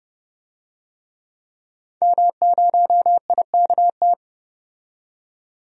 Morse code, heard as a keyed tone, about 30 wpm, 700 Hz. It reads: M0IKT
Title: DXCC-100 Call Signs - 5 Characters - 30wpm